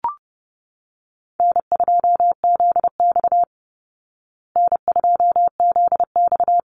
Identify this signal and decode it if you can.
Morse code, heard as a keyed tone, about 30 wpm, 700 Hz. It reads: N2ZX N2ZX
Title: Call Signs Repeated 2x - 30wpm